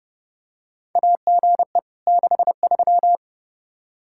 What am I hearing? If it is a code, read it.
Morse code, heard as a keyed tone, about 30 wpm, 700 Hz. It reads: AGE 63
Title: Practice QSO - Exchange 3 - More Info - 30wpm